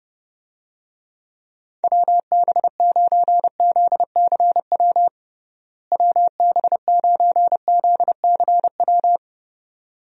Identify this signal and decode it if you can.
Morse code, heard as a keyed tone, about 30 wpm, 700 Hz. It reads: WB9ZCW WB9ZCW
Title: USA Call Signs - 6 Characters - Repeated 2x - 30wpm